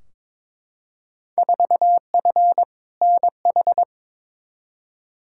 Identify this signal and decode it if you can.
Morse code, heard as a keyed tone, about 22 wpm, 700 Hz. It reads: 4F NH
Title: Field Day Exchanges Encore - 22wpm